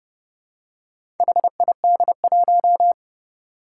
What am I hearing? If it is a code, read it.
Morse code, heard as a keyed tone, about 30 wpm, 700 Hz. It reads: HID1
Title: Four Challenging Letter-Number Combinations 30wpm